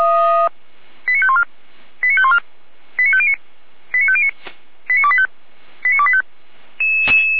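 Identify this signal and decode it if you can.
Five-tone selective call (ZVEI), heard as quick runs of five tones, tones of about 70 ms. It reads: 89514 89514 89509 89509 89285 89285
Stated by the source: ZVEI five-tone selective call 89150 received by SDR.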